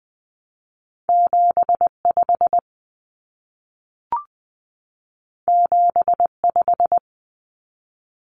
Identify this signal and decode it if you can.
Morse code, heard as a keyed tone, about 20 wpm, 700 Hz. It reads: 75 75
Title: Common Cut Numbers - 2-Digit 20wpm